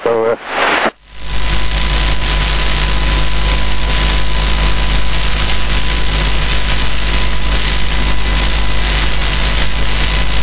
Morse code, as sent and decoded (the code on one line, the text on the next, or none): none